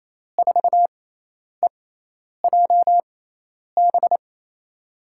Morse code, heard as a keyed tone, about 28 wpm, spaced as for 10 wpm, 700 Hz. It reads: 4EJB
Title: USA Call Signs - 6x Character Spacing - 28wpm